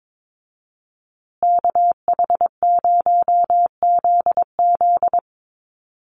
Morse code, heard as a keyed tone, about 22 wpm, 700 Hz. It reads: KH0ZZ
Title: Call Signs Repeated 2x - 22wpm